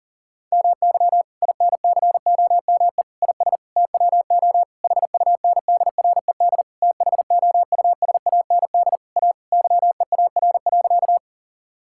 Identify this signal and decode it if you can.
Morse code, heard as a keyed tone, about 40 wpm, 700 Hz. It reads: MY INCOME IS TWO HUNDRED THOUSAND A YEAR.
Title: Sentences from Top 1000 Words - Review Part 5 - 40wpm